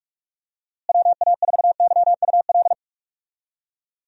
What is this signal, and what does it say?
Morse code, heard as a keyed tone, about 46 wpm, 700 Hz. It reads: WA4YUL
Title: USA Call Signs - 6 Characters - Repeated 3x - 45wpm